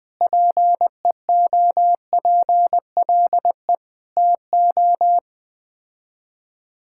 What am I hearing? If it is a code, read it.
Morse code, heard as a keyed tone, about 21 wpm, 700 Hz. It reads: PEOPLE TO
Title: Sets of 6 Words - 20wpm